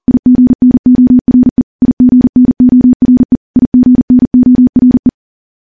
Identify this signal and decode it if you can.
Morse code, heard as a keyed tone, about 40 wpm, 250 Hz. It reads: IGNORE IGNORE IGNORE
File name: morse / morse-training-40WPM